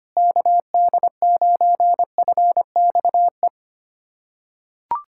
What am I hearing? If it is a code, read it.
Morse code, heard as a keyed tone, about 25 wpm, 690 Hz. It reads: KD9FXE
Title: SOTA QSO Practice 25wpm